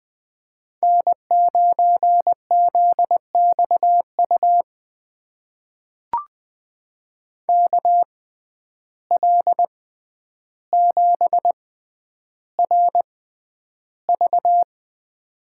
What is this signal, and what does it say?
Morse code, heard as a keyed tone, about 21 wpm, spaced as for 10 wpm, 700 Hz. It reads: N9ZXU KL7RV
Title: USA Call Signs - 5 Characters - 6x Character Spacing - 20wpm